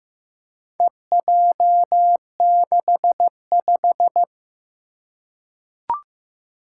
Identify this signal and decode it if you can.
Morse code, heard as a keyed tone, about 15 wpm, 700 Hz. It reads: EJ65
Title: Four Challenging Letter-Number Combinations 15wpm